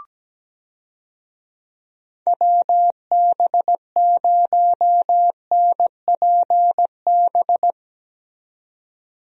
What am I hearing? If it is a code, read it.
Morse code, heard as a keyed tone, about 17 wpm, 700 Hz. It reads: WB0NPB
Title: USA Call Signs - 6 Characters - 17wpm